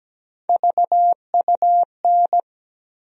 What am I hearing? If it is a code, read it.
Morse code, heard as a keyed tone, about 17 wpm, 700 Hz. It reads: VUN
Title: USA Call Sign Postfixes - 3 Letters 17wpm